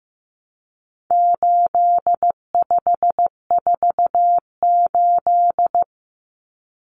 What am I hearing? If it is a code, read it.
Morse code, heard as a keyed tone, about 15 wpm, 700 Hz. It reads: 8548
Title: Numbers - 4-Digit 15wpm